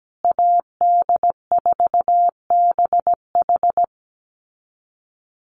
Morse code, heard as a keyed tone, about 17 wpm, 700 Hz. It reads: AD4BH
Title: USA Call Signs Repeated 2x 17wpm